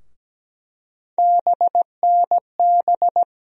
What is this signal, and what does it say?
Morse code, heard as a keyed tone, about 17 wpm, 700 Hz. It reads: BNB
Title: USA Call Sign Postfixes - 3 Letters 17wpm